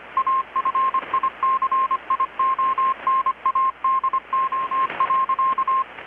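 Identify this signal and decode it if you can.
Morse code, heard as a keyed tone, about 25 wpm, 1060 Hz, with radio noise. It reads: AFICIONADO.
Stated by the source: Morse from KPH's Night of Nights XX maritime radiotelegraphy event, received in Chiapas, Mexico about 3,800 km away on a Kenwood R-600 with a grounded 10 m long-wire antenna.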